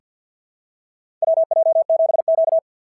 Morse code, heard as a keyed tone, about 48 wpm, 690 Hz. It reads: WJ7Q
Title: SOTA QSO Practice 50wpm